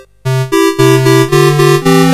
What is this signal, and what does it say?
There is no code.